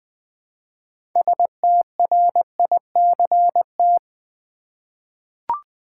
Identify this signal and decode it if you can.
Morse code, heard as a keyed tone, about 21 wpm, 700 Hz. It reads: STRICT
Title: Six Letter Words - 20wpm